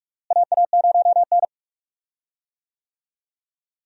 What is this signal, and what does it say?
Morse code, heard as a keyed tone, about 41 wpm, 700 Hz. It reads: AA0N
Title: USA Call Signs - 4 Characters - Repeated 2x - 45wpm